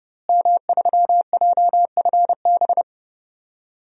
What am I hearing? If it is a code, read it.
Morse code, heard as a keyed tone, about 32 wpm, 700 Hz. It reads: M3JFB
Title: DXCC-100 Call Signs - 5 Characters - Repeated 2x - 30wpm